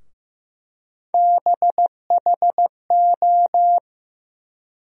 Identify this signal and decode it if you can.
Morse code, heard as a keyed tone, about 15 wpm, 700 Hz. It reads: BHO
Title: USA Call Sign Postfixes - 3 Letters 15wpm